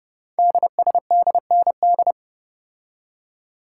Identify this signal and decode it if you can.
Morse code, heard as a keyed tone, about 30 wpm, 700 Hz. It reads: DSDND
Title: Code Groups - 5 Letters - 30wpm